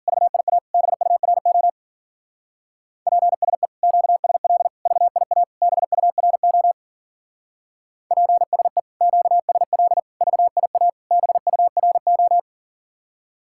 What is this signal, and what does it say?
Morse code, heard as a keyed tone, about 47 wpm, 700 Hz. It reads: VIA BURO PSE QSL VIA BURO PSE QSL VIA BURO
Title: Speed-Racing Common QSO Phrases - 40wpm